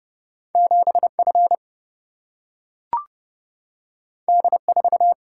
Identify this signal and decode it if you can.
Morse code, heard as a keyed tone, about 29 wpm, 700 Hz. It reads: 7F D4
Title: Single-Letter Single-Number 30wpm